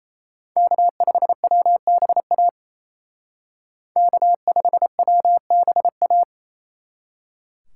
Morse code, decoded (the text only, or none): K5WBA K5WBA